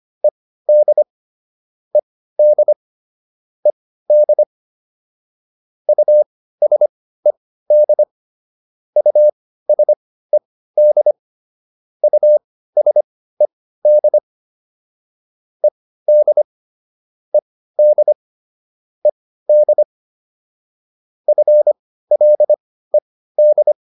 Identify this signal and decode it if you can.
Morse code, heard as a keyed tone, about 25 wpm, spaced as for 15 wpm, 600 Hz. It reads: ED ED ED USED USED USED ED ED ED FLED